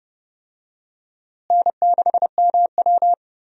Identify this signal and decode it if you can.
Morse code, heard as a keyed tone, about 30 wpm, 700 Hz. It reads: N6MW
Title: USA Call Signs - 4 Characters - Repeated 3x - 30wpm